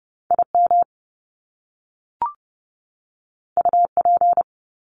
Morse code, heard as a keyed tone, about 30 wpm, 700 Hz. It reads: IM UP
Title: Top Two Letter Combinations in Words 30wpm